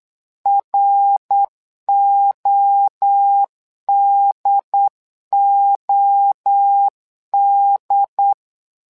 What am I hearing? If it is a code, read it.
Morse code, heard as a keyed tone, about 8 wpm, 800 Hz. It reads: RODOD
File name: LEZIONI / 3 MWRO / 3 MWRO 8wpm 2606